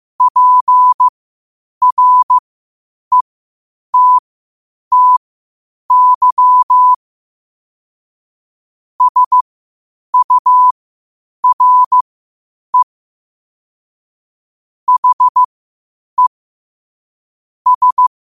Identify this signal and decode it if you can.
Morse code, heard as a keyed tone, about 15 wpm, spaced as for 8 wpm, 1000 Hz. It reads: PRETTY SURE HE S
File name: qotdfiles / 8wpm